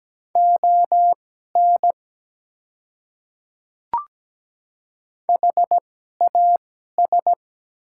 Morse code, heard as a keyed tone, about 17 wpm, spaced as for 12 wpm, 700 Hz. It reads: ON HAS
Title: 2x Character Spacing - Top 200 Words - 17wpm